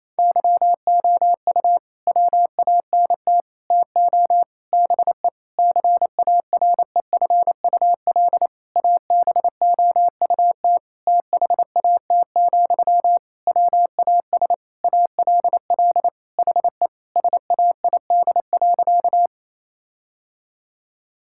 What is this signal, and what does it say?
Morse code, heard as a keyed tone, about 27 wpm, 700 Hz. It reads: YOU WANT TO BE CAREFUL ABOUT THAT, WAS ALL HE SAID.
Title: Sentences from Top 1600 Words 28wpm